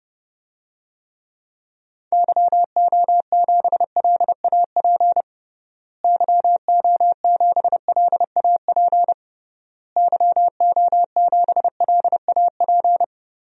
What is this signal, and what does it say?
Morse code, heard as a keyed tone, about 30 wpm, 700 Hz. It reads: YO7LAP YO7LAP YO7LAP
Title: DXCC-100 Call Signs - 6 Characters - Repeated 3x - 30wpm